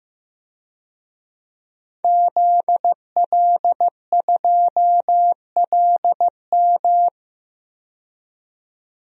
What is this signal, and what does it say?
Morse code, heard as a keyed tone, about 15 wpm, 700 Hz. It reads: ZL2LM
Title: DXCC-100 Call Signs - 5 Characters - 15wpm